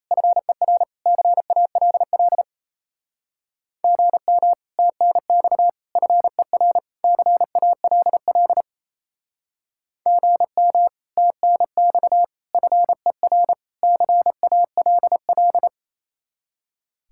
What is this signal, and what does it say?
Morse code, heard as a keyed tone, about 35 wpm, 700 Hz. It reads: FER CALL GM TNX FER CALL GM TNX FER CALL